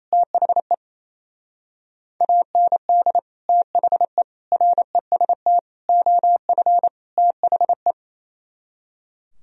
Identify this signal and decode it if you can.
Morse code, heard as a keyed tone, about 37 wpm, 700 Hz. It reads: THE AND THE REST OF THE